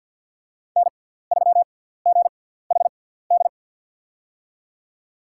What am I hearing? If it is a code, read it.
Morse code, heard as a keyed tone, about 50 wpm, spaced as for 17 wpm, 700 Hz. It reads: N3GHD
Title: USA Call Signs - 5 Characters - 6x Character Spacing - 50wpm